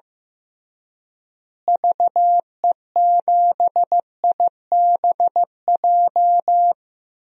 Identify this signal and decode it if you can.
Morse code, heard as a keyed tone, about 15 wpm, 700 Hz. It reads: VE7IBJ
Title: DXCC-100 Call Signs - 6 Characters - Repeated 3x - 15wpm